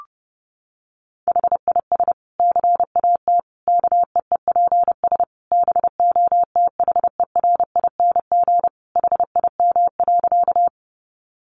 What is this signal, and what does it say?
Morse code, heard as a keyed tone, about 28 wpm, 700 Hz. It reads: HIS CAT KEEPS BOTHERING HIM.
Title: Additional Easy Phrases 30wpm